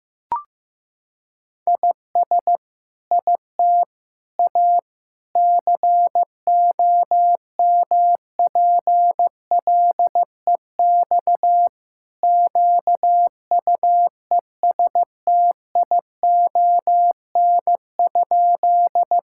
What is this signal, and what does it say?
Morse code, heard as a keyed tone, about 15 wpm, 700 Hz. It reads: IS IT A COMPLEX QUESTION?
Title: Sentences from Top 1000 Words - Review Part 6 - 15wpm